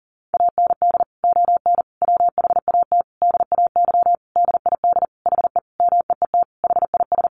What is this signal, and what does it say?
Morse code, heard as a keyed tone, about 40 wpm, 700 Hz. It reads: AND ON WHAT DAY DID HE MEET HIS
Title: Sentences from Top 1100 Words 40wpm